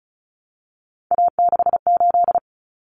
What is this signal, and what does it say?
Morse code, heard as a keyed tone, about 34 wpm, 700 Hz. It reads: A68
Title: Common Cut Numbers - 3-Digit 35wpm